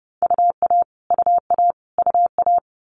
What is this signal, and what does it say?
Morse code, heard as a keyed tone, about 30 wpm, 700 Hz. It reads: UA UA UA
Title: Multi-letter Phonemes Repeated 3x 30wpm